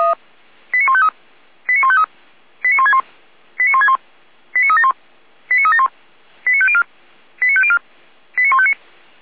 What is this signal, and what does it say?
Five-tone selective call (ZVEI), heard as quick runs of five tones, tones of about 70 ms. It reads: 89153 89153 89161 89161 89371 89371 89504 89504 89159